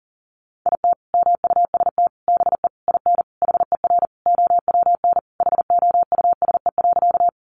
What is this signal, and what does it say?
Morse code, heard as a keyed tone, about 40 wpm, 700 Hz. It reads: IT MUST BE IN HER OWN HOUSE.
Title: Sentences from Top 300 Words 40wpm